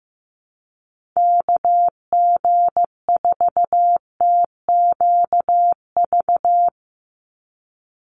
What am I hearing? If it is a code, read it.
Morse code, heard as a keyed tone, about 15 wpm, 700 Hz. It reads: KG4TQV
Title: USA Call Signs 15wpm